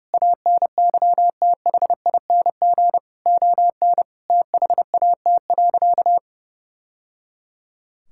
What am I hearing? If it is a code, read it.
Morse code, heard as a keyed tone, about 30 wpm, 700 Hz. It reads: ANYTHING ON THAT.
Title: Sentences from Top 1000 Words - Review Part 5 - 30wpm